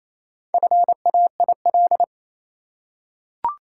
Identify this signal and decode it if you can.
Morse code, heard as a keyed tone, about 28 wpm, 700 Hz. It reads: FAIL
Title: Four Letter Words - 28wpm